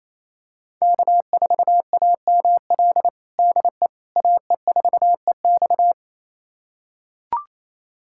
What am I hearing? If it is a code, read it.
Morse code, heard as a keyed tone, about 28 wpm, 700 Hz. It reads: K4AML DE AE4EX
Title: USA Callsign DE Callsign - 5 Characters - 28wpm